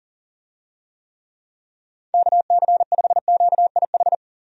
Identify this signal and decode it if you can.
Morse code, heard as a keyed tone, about 40 wpm, 700 Hz. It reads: KC5QIH
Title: USA Call Signs - 6 Characters - 40wpm